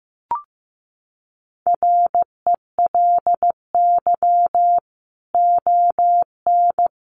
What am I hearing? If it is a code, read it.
Morse code, heard as a keyed tone, about 15 wpm, 700 Hz. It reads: RELY ON
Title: Sentences from Top 1200 Words 15wpm